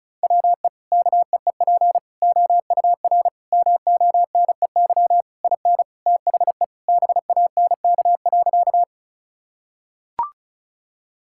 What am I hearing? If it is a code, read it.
Morse code, heard as a keyed tone, about 36 wpm, 700 Hz. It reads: WE KEEP OUR MONEY IN THE BANK.
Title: Sentences from Top 1000 Words - Review Part 6 - 35wpm